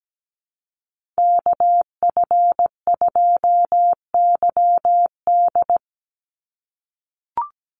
Morse code, heard as a keyed tone, about 17 wpm, 700 Hz. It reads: KF2YD
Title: USA Call Signs 17wpm